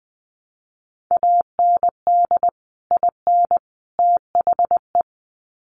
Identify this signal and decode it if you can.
Morse code, heard as a keyed tone, about 20 wpm, 700 Hz. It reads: AND IN THE
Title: Sets of 3 Words 20wpm